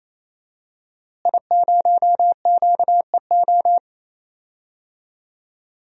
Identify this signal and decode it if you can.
Morse code, heard as a keyed tone, about 28 wpm, 700 Hz. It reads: I0QEO
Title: DXCC-100 Call Signs - 5 Characters - Repeated 3x - 28wpm